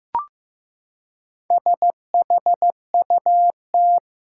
Morse code, heard as a keyed tone, about 16 wpm, 700 Hz. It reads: SHUT
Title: Four Letter Words - 15wpm